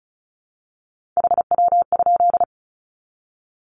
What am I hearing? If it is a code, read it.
Morse code, heard as a keyed tone, about 34 wpm, 700 Hz. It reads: HW?
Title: Common QSO Phrases 35wpm